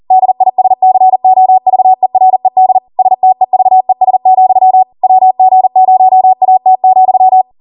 Morse code, heard as a keyed tone, about 40 wpm, 750 Hz. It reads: DISCOVERED STEVES, WG0AT,